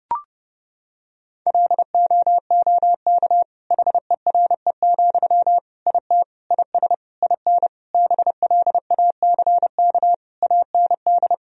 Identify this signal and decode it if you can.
Morse code, heard as a keyed tone, about 32 wpm, 700 Hz. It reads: LOOK HERE, IT IS IN BLACK AND
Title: Sentences from Top 1000 Words - Review Part 2 - 30wpm